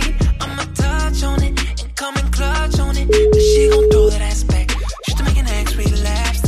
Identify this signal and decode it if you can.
Ringback tone (Europe, 425 Hz), with noise.